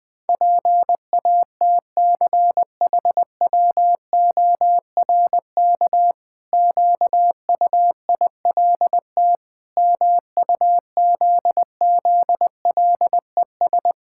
Morse code, heard as a keyed tone, about 20 wpm, 700 Hz. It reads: PATCHWORK QUILT MUZZLES
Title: Pangram Challenge v2 - 20wpm